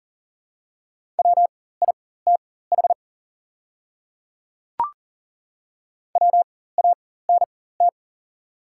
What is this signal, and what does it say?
Morse code, heard as a keyed tone, about 41 wpm, spaced as for 19 wpm, 700 Hz. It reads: WITH WANT
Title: 4x Character Spacing - Top 200 Words - 40wpm